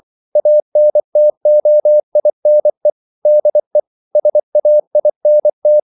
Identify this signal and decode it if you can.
Morse code, heard as a keyed tone, about 24 wpm, 600 Hz. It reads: ANTOINE DE SAINT
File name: CW / Practice / EnglishQuotations0224